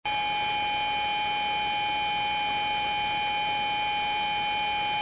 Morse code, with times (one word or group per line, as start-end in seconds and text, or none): none